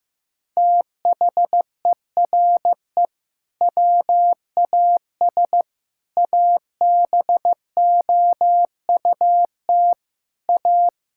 Morse code, heard as a keyed tone, about 15 wpm, 700 Hz. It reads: THERE WAS ABOUT A
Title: Sentences from Top 1000 Words - Review Part 5 - 15wpm